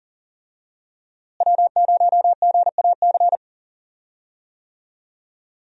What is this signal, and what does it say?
Morse code, heard as a keyed tone, about 40 wpm, 700 Hz. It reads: W0GAC
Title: USA Call Signs - 5 Characters - Repeated 2x - 40wpm